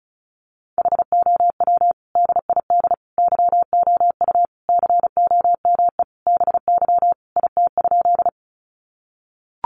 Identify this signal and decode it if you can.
Morse code, heard as a keyed tone, about 34 wpm, 700 Hz. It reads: HOW DID YOU COME BY IT?
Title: Sentences from Top 100 Words 35wpm